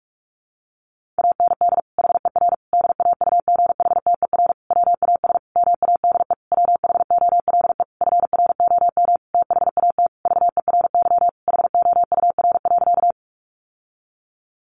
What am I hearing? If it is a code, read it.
Morse code, heard as a keyed tone, about 44 wpm, 700 Hz. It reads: AND HER DAUGHTER WAS MADE WHOLE FROM THAT VERY HOUR.
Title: Sentences from Top 1500 Words 45wpm